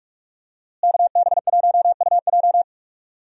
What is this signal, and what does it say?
Morse code, heard as a keyed tone, about 42 wpm, 700 Hz. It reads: KB1UJ
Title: USA Call Signs - 5 Characters - Repeated 3x - 45wpm